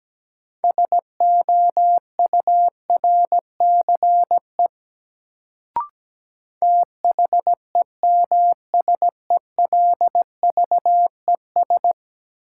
Morse code, heard as a keyed tone, about 17 wpm, 700 Hz. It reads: SOURCE THEMSELVES
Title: Sentences from Top 800 Words - New Word Drill - 17wpm